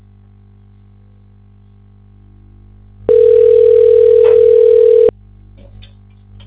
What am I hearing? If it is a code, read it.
Ringback tone (North America), 440 + 480 Hz.